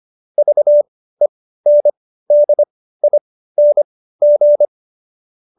Morse code, heard as a keyed tone, about 25 wpm, spaced as for 15 wpm, 590 Hz. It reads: VENDING
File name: Jokes-25-char-15-eff